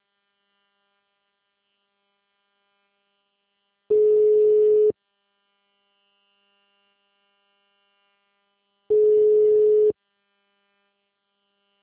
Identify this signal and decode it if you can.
Ringback tone (Europe, 425 Hz).